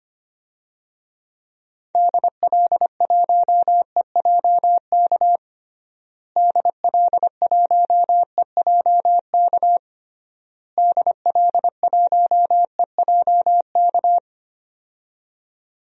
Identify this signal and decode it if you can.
Morse code, heard as a keyed tone, about 25 wpm, 700 Hz. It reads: DL1EJK DL1EJK DL1EJK
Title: DXCC-100 Call Signs - 6 Characters - Repeated 3x - 25wpm